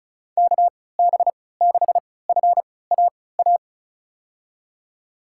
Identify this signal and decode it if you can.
Morse code, heard as a keyed tone, about 36 wpm, spaced as for 20 wpm, 700 Hz. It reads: KB6FAA
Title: USA Call Signs - 6 Characters - 3x Character Spacing - 35wpm